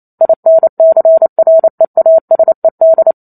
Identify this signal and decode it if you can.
Morse code, heard as a keyed tone, about 28 wpm, 650 Hz. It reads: INCREASED
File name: Chapter 2 - 28 WPM